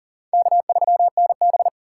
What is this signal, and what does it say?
Morse code, heard as a keyed tone, about 40 wpm, 700 Hz. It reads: K3NB
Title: USA Call Signs - 4 Characters - 3x Character Spacing - 40wpm